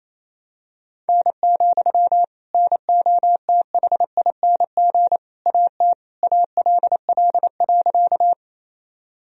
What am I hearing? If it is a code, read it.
Morse code, heard as a keyed tone, about 28 wpm, 700 Hz. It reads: N, NOTHING AT ALL.
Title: Sentences from Top 500 Words and Abbreviations 28wpm